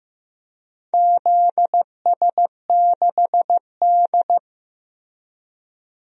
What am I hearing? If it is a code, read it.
Morse code, heard as a keyed tone, about 15 wpm, 700 Hz. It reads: ZS6D
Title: DXCC-100 Call Signs - 4 Characters - 15wpm